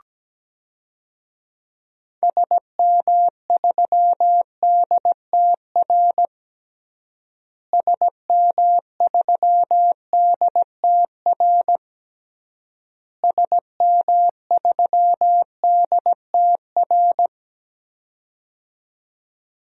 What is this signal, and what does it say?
Morse code, heard as a keyed tone, about 17 wpm, 700 Hz. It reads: SM3DTR SM3DTR SM3DTR
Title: DXCC-100 Call Signs - 6 Characters - Repeated 3x - 17wpm